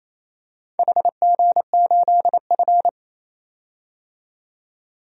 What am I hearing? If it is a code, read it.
Morse code, heard as a keyed tone, about 27 wpm, 700 Hz. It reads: HG8F
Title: DXCC-100 Call Signs - 4 Characters - 28wpm